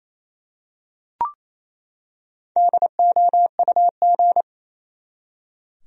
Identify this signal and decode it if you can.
Morse code, heard as a keyed tone, about 28 wpm, 700 Hz. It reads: DOUG
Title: USA Common Names - 28wpm